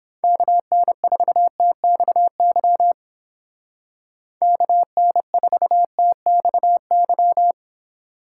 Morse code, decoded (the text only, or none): KN4TXY KN4TXY